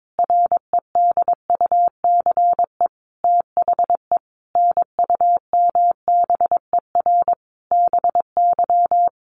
Morse code, heard as a keyed tone, about 22 wpm, 700 Hz. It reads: REDUCE THE NUMBER BY
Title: Sentences from Top 1100 Words 22wpm